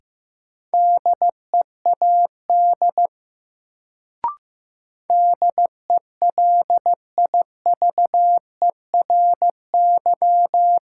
Morse code, heard as a keyed tone, about 15 wpm, 700 Hz. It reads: DEAD DELIVERY
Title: Sentences from Top 1500 Words - New Word Drill Alt - 15wpm